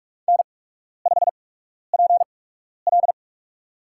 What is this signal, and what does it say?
Morse code, heard as a keyed tone, about 46 wpm, spaced as for 12 wpm, 700 Hz. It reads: N5PL
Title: USA Call Signs - 8x Character Spacing - 45wpm